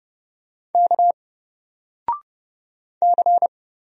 Morse code, heard as a keyed tone, about 30 wpm, 700 Hz. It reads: K C
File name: Single-Letter-Number-30wpm